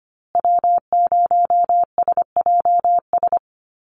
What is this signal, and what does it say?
Morse code, heard as a keyed tone, about 25 wpm, 700 Hz. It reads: W0SJS
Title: Call Signs Repeated 2x - 25wpm